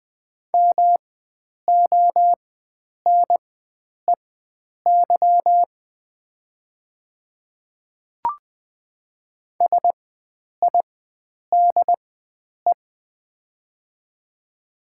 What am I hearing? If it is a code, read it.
Morse code, heard as a keyed tone, about 20 wpm, spaced as for 9 wpm, 690 Hz. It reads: MONEY SIDE
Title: 4x Character Spacing - Top 500 Words - 20wpm